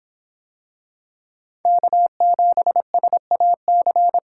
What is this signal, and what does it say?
Morse code, heard as a keyed tone, about 24 wpm, 700 Hz. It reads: K7SAC